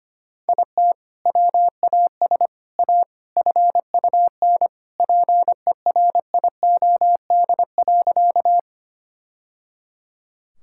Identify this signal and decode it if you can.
Morse code, heard as a keyed tone, about 25 wpm, 700 Hz. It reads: IT WAS A FUN PERIOD.
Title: Sentences from Top 1000 Words - Review Part 5 - 25wpm